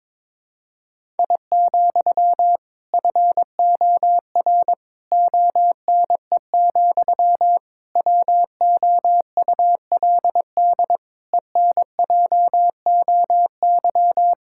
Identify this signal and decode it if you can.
Morse code, heard as a keyed tone, about 22 wpm, 700 Hz. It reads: I, FOR ONE, WOULD ENJOY
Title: Sentences from Top 1000 Words - Review Part 5 - 22wpm